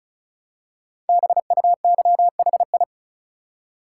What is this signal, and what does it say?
Morse code, heard as a keyed tone, about 36 wpm, 700 Hz. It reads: BUYHI
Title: Code Groups - 5 Letters - 35wpm